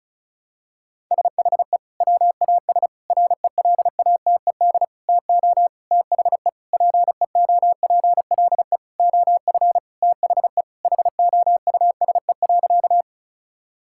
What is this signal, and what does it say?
Morse code, heard as a keyed tone, about 36 wpm, 700 Hz. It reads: SHE WAS RELATED TO THE PEOPLE OF THE HOUSE.
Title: Sentences from Top 1000 Words - Review Part 5 - 35wpm